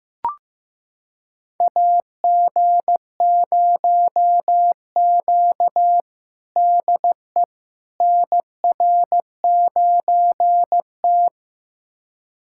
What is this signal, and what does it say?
Morse code, heard as a keyed tone, about 15 wpm, 700 Hz. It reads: AG0Q DE NR9T
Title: USA Callsign DE Callsign - 4 Characters - 15wpm